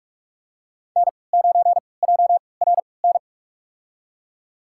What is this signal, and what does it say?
Morse code, heard as a keyed tone, about 44 wpm, spaced as for 25 wpm, 700 Hz. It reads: N9JRN
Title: USA Call Signs - 5 Characters - 3x Character Spacing - 45wpm